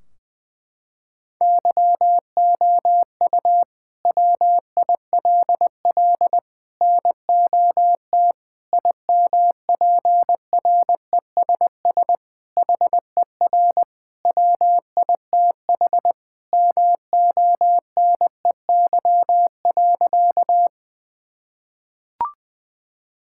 Morse code, heard as a keyed tone, about 21 wpm, 700 Hz. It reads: YOU WILL NOT IMPRESS HER WITH MONEY.
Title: Sentences from Top 1600 Words 20wpm